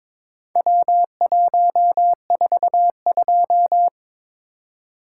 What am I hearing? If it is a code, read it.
Morse code, heard as a keyed tone, about 22 wpm, 700 Hz. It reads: W142
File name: practice / 22wpm / Four Letter-Number Combinations - 22wpm